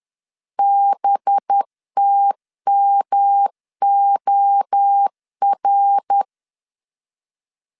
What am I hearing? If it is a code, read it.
Morse code, heard as a keyed tone, about 10 wpm, 800 Hz. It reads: BTMOR